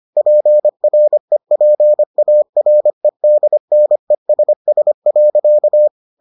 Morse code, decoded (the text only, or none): PREPAREDNESS.